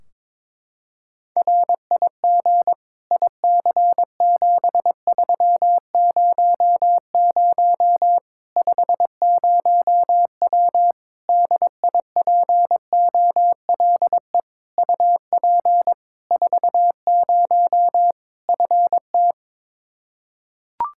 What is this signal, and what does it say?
Morse code, heard as a keyed tone, about 22 wpm, 700 Hz. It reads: RIG IC7300 50W DIPOLE UP 40 FT
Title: Practice QSO - Exchange 2 - Info - 22wpm